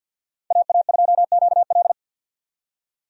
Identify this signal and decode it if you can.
Morse code, heard as a keyed tone, about 45 wpm, 700 Hz. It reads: AA2QL
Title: USA Call Signs - 5 Characters - 50wpm